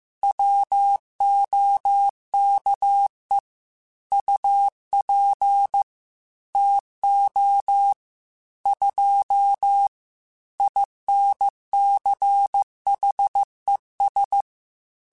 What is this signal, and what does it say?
Morse code, heard as a keyed tone, about 15 wpm, 800 Hz. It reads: WOKE UP TO 2 INCHES
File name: Weather Sets / Weather Sets-mp3s / Weather 3 17 @13